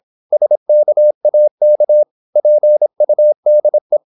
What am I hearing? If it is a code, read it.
Morse code, heard as a keyed tone, about 26 wpm, 600 Hz. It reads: SKAK PUDE
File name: Korte ord DK 26wpm